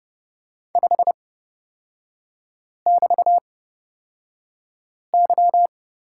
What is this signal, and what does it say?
Morse code, heard as a keyed tone, about 30 wpm, spaced as for 5 wpm, 700 Hz. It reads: 5<BT>Y
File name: Lesson 37e - Letter x - Three Characters 30wpm